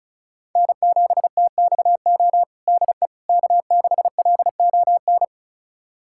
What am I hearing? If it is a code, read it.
Morse code, heard as a keyed tone, about 36 wpm, 700 Hz. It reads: N7TXO DE K6LON
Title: USA Callsign DE Callsign - 5 Characters - 35wpm